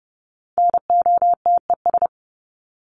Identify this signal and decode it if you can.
Morse code, heard as a keyed tone, about 30 wpm, 700 Hz. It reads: NOTES
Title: ES:S Suffix - Top 100 Words with Base Words 30wpm